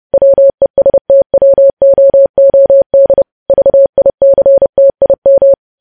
Morse code, heard as a keyed tone, about 30 wpm, 550 Hz. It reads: WESTWOOD VICTIM
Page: news headlines in morse code at 30 wpm